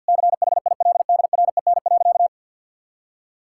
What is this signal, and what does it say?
Morse code, heard as a keyed tone, about 52 wpm, 700 Hz. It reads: CHILDREN.
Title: Sentences from Top 1000 Words - Review Part 5 - 50wpm